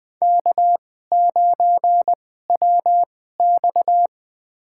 Morse code, heard as a keyed tone, about 20 wpm, spaced as for 15 wpm, 700 Hz. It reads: K9WX
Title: USA Call Signs - 2x Character Spacing - 20wpm